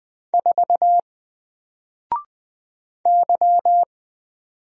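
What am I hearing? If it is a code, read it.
Morse code, heard as a keyed tone, about 20 wpm, 700 Hz. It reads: 4 Y
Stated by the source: Single letter and number Morse drill at 20 wpm, from N4LCD.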